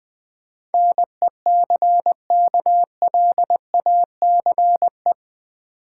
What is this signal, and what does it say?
Morse code, heard as a keyed tone, about 22 wpm, 700 Hz. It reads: NECKLACE
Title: Difficult Words - Speed-Racing with Hints - 20wpm